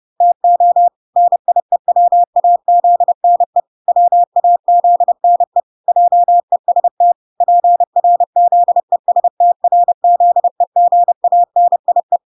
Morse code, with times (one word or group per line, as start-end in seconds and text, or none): 0.2-0.9 TO
1.2-3.6 NIEWAZNE
3.9-5.6 WAZNE
5.9-7.1 JEST
7.4-12.2 PRZESTRZEGANIE